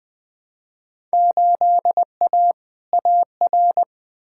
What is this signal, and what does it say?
Morse code, heard as a keyed tone, about 20 wpm, 700 Hz. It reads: 8A AR
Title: Field Day Exchanges Encore - 20wpm